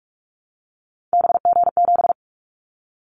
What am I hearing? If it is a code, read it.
Morse code, heard as a keyed tone, about 46 wpm, 700 Hz. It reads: BG7
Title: Callsign Prefixes for Top 100 DXCC - 45wpm